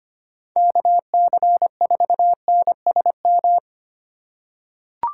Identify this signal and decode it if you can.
Morse code, heard as a keyed tone, about 25 wpm, 700 Hz. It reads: KC4NSM
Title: USA Call Signs - 6 Characters - 3x Character Spacing - 25wpm